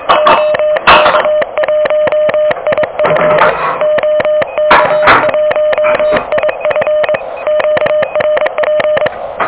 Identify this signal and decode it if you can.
Morse code, heard as a keyed tone, about 23 wpm, 600 Hz, with noise. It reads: OK1IF OK1IF QRP
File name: OK QRP 2023 / OK QRP 2023